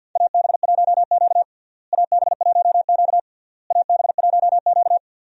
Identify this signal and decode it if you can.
Morse code, heard as a keyed tone, about 51 wpm, 700 Hz. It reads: AB1Q AB1Q AB1Q
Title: USA Call Signs - 4 Characters - Repeated 3x - 50wpm